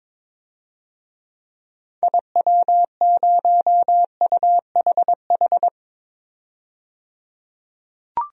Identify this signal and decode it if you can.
Morse code, heard as a keyed tone, about 22 wpm, 700 Hz. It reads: IW0UHH
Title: DXCC-100 Call Signs - 6 Characters - 22wpm